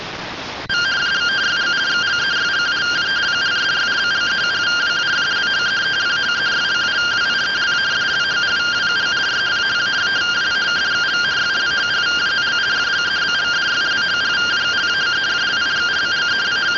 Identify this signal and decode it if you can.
Two-tone teleprinter signal, about 170 Hz shift.